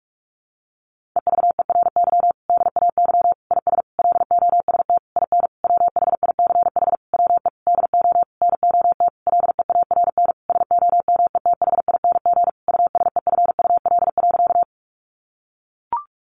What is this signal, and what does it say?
Morse code, heard as a keyed tone, about 44 wpm, 700 Hz. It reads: EVERY DAY IS LOST IN WHICH WE DO NOT LEARN SOMETHING USEFUL.
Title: Sentences from Top 1400 Words 45wpm